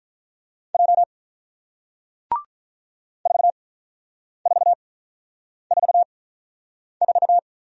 Morse code, heard as a keyed tone, about 48 wpm, spaced as for 9 wpm, 700 Hz. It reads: J 4444